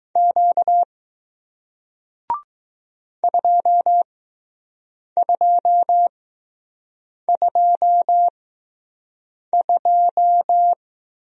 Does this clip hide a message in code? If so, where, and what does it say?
Yes: Morse code from 0.2 to 10.7 s, heard as a keyed tone, about 20 wpm, spaced as for 6 wpm, 700 Hz. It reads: Q 2222